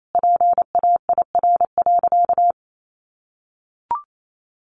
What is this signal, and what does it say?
Morse code, heard as a keyed tone, about 28 wpm, 700 Hz. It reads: PAIR.
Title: Sentences from Top 1200 Words 28wpm